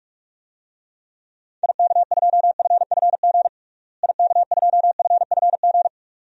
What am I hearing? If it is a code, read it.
Morse code, heard as a keyed tone, about 46 wpm, 700 Hz. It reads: IK2FFG IK2FFG
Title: DXCC-100 Call Signs - 6 Characters - Repeated 2x - 45wpm